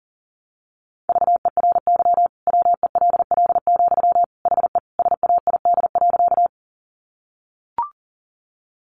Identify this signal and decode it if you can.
Morse code, heard as a keyed tone, about 37 wpm, 700 Hz. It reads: VERY WELL, HE SAID.
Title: Sentences from Top 200 Words 40wpm